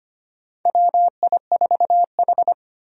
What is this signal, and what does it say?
Morse code, heard as a keyed tone, about 25 wpm, 700 Hz. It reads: WI4H
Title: USA Call Signs - 4 Characters - Repeated 2x - 25wpm